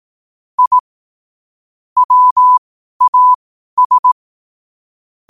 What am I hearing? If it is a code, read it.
Morse code, heard as a keyed tone, about 18 wpm, spaced as for 12 wpm, 1000 Hz. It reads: I WAS
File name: qotdfiles / 12wpm